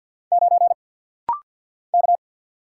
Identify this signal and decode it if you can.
Morse code, heard as a keyed tone, about 49 wpm, 700 Hz. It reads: 9 K
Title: Single Letter-Number Alt - Rapid-Fire 50wpm